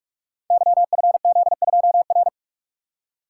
Morse code, heard as a keyed tone, about 44 wpm, 700 Hz. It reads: YFZ2R
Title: Code Groups - 5 Letter:Numbers - 45wpm